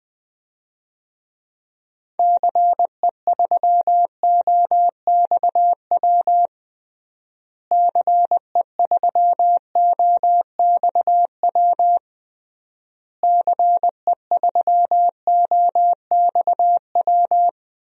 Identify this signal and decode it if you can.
Morse code, heard as a keyed tone, about 20 wpm, 700 Hz. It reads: CE3OXW CE3OXW CE3OXW
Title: DXCC-100 Call Signs - 6 Characters - Repeated 3x - 20wpm